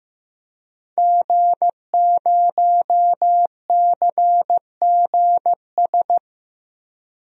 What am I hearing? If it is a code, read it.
Morse code, heard as a keyed tone, about 15 wpm, 700 Hz. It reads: G0CGS
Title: DXCC-100 Call Signs - 5 Characters - 15wpm